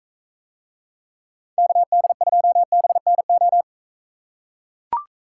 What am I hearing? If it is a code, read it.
Morse code, heard as a keyed tone, about 41 wpm, 700 Hz. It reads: KD2BNO